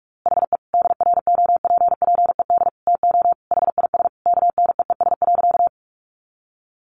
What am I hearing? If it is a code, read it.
Morse code, heard as a keyed tone, about 45 wpm, 700 Hz. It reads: HE DROPPED TO HIS KNEES.
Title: A Sound Of Thunder 45wpm